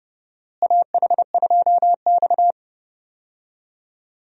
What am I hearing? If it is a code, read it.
Morse code, heard as a keyed tone, about 30 wpm, 700 Hz. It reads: AH2X
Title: USA Call Signs - 4 Characters - Repeated 2x - 30wpm